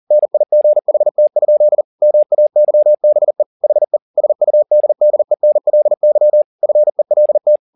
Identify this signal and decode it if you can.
Morse code, heard as a keyed tone, about 40 wpm, 600 Hz. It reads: NIGHT? MAYBE HE SUDDENLY FELT
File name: the old man and the sea 2 40wpm